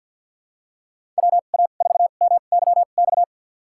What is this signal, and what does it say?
Morse code, heard as a keyed tone, about 45 wpm, spaced as for 34 wpm, 700 Hz. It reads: WA4MYX
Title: USA Call Signs - 2x Character Spacing - 50wpm